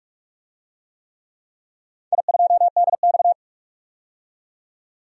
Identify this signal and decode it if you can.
Morse code, heard as a keyed tone, about 44 wpm, 700 Hz. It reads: I2DX
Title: DXCC-100 Call Signs - 4 Characters - 45wpm